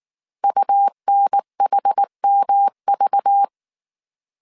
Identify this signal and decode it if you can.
Morse code, heard as a keyed tone, about 19 wpm, 800 Hz. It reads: UNHMV
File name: CW / DET / 1 6 1500C S A 18wpm 2255